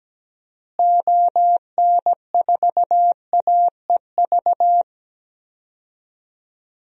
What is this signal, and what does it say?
Morse code, heard as a keyed tone, about 17 wpm, 700 Hz. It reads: ON4AEV
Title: DXCC-100 Call Signs - 6 Characters - Repeated 3x - 17wpm